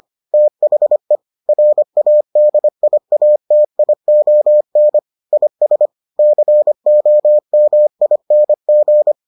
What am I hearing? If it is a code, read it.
Morse code, heard as a keyed tone, about 25 wpm, 600 Hz. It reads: THE RADIATION IS COMING